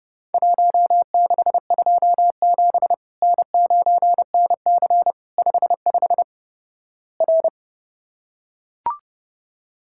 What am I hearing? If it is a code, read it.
Morse code, heard as a keyed tone, about 31 wpm, 700 Hz. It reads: 1627 N9NC 55 R
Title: Worked All Europe DX Contest - QTC Exchange - 30wpm